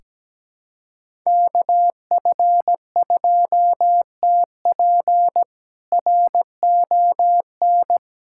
Morse code, heard as a keyed tone, about 17 wpm, 700 Hz. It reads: KF2TP RON
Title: K1USN SST QSO Practice - 17wpm